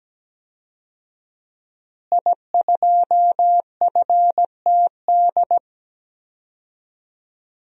Morse code, heard as a keyed tone, about 17 wpm, 700 Hz. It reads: I2FTD
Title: DXCC-100 Call Signs - 5 Characters - Repeated 3x - 17wpm